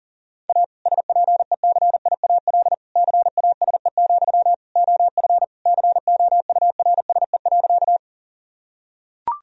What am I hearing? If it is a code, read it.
Morse code, heard as a keyed tone, about 40 wpm, 700 Hz. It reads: A SPECIAL CASE, OF COURSE.
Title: Sentences from Top 1000 Words - Review Part 3 - 40wpm